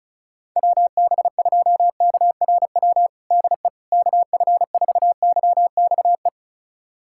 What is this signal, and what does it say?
Morse code, heard as a keyed tone, about 36 wpm, 700 Hz. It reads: WB2KRW DE KF4YXE
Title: USA Callsign DE Callsign - 6 Characters - 35wpm